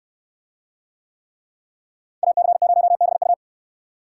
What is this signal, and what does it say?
Morse code, heard as a keyed tone, about 68 wpm, 700 Hz. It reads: AC0BU